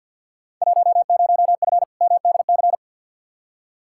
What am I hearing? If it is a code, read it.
Morse code, heard as a keyed tone, about 53 wpm, 700 Hz. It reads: 10F MDC
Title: Field Day Exchanges Encore - 50wpm